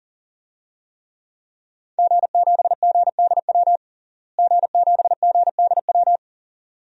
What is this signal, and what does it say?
Morse code, heard as a keyed tone, about 40 wpm, 700 Hz. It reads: G7GDW G7GDW
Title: DXCC-100 Call Signs - 5 Characters - Repeated 2x - 40wpm